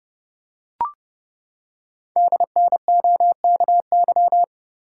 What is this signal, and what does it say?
Morse code, heard as a keyed tone, about 30 wpm, 690 Hz. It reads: DNOKY
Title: Code Groups - 5 Letters - 30wpm